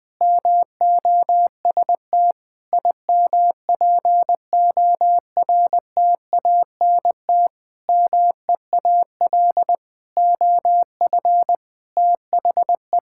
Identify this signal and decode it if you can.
Morse code, heard as a keyed tone, about 20 wpm, 700 Hz. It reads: MOST IMPORTANT MEAL OF THE
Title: Sentences with 9-Word Phrases - 20wpm